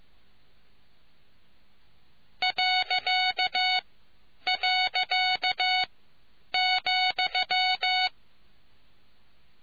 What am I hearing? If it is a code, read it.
Morse code, heard as a keyed tone, about 15 wpm, spaced as for 9 wpm, 740 Hz. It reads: ..,